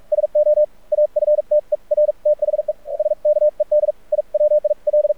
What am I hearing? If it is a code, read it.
Morse code, heard as a keyed tone, about 46 wpm, spaced as for 40 wpm, 600 Hz. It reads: SO AFTER THE SKED I WIL
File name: archived / CRSnet